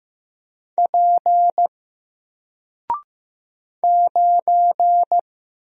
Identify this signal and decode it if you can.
Morse code, heard as a keyed tone, about 15 wpm, 700 Hz. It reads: P 9
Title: Single Letter-Number Alt 15wpm